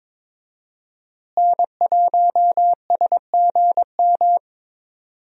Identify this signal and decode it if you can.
Morse code, heard as a keyed tone, about 22 wpm, 700 Hz. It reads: N1SGM
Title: USA Call Signs - 5 Characters - Repeated 2x - 22wpm